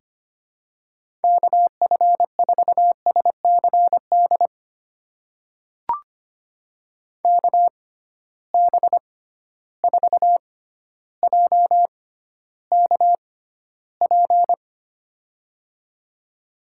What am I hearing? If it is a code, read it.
Morse code, heard as a keyed tone, about 25 wpm, spaced as for 13 wpm, 700 Hz. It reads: KF4SCD KB4JKP
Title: USA Call Signs - 6 Characters - 6x Character Spacing - 25wpm